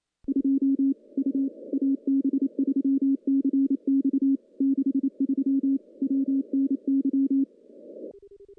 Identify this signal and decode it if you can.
Morse code, heard as a keyed tone, about 28 wpm, 270 Hz. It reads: 2 U AB3CX 63 WNY
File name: contests / CW-SS-k5tr- -1300